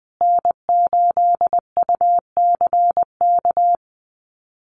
Morse code, heard as a keyed tone, about 20 wpm, 700 Hz. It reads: N8UCK
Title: USA Call Signs Repeated 2x 20wpm